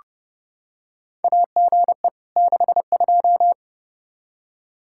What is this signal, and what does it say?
Morse code, heard as a keyed tone, about 30 wpm, 700 Hz. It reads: AGE 62
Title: Practice QSO - Exchange 3 - More Info - 30wpm